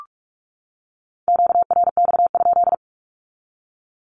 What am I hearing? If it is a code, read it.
Morse code, heard as a keyed tone, about 44 wpm, 700 Hz. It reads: QRX?
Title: Q-Codes 45wpm